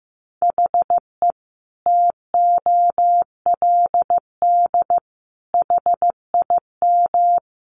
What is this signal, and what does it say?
Morse code, heard as a keyed tone, about 15 wpm, 700 Hz. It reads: HE TOLD HIM
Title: Sentences from Top 700 Words 15wpm